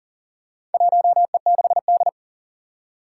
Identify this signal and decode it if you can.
Morse code, heard as a keyed tone, about 37 wpm, 700 Hz. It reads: 1E6D
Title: Four Challenging Letter-Number Combinations 40wpm